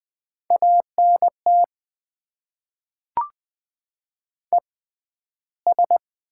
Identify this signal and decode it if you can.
Morse code, heard as a keyed tone, about 20 wpm, spaced as for 12 wpm, 690 Hz. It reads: ANT ES
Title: Top 20 QSO Elements - 6x Character Spacing - 20wpm